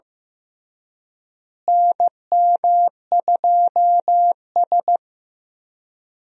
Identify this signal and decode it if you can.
Morse code, heard as a keyed tone, about 15 wpm, 700 Hz. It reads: NM2S
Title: USA Call Signs - 4 Characters - Repeated 3x - 15wpm